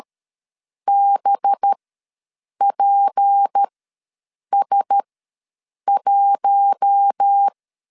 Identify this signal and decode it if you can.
Morse code, heard as a keyed tone, about 13 wpm, 800 Hz. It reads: B P S 1
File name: LEZIONI / 6 PKX0 / 6 PKX ws 12wpm 3106